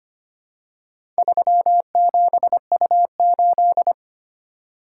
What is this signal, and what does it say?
Morse code, heard as a keyed tone, about 25 wpm, 700 Hz. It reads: 37U8
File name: practice / 25wpm / Four Letter-Number Combinations - 25wpm